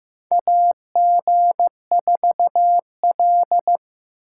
Morse code, heard as a keyed tone, about 15 wpm, 700 Hz. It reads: AG4L
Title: USA Call Signs - 3x Character Spacing - 15wpm